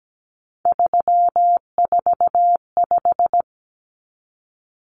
Morse code, heard as a keyed tone, about 17 wpm, 700 Hz. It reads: 345
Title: Common Cut Numbers - 3-Digit 17wpm